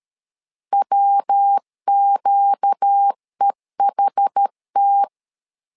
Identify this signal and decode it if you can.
Morse code, heard as a keyed tone, about 13 wpm, 800 Hz. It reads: WQEHT
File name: LEZIONI / 4 Q 12wpm 1833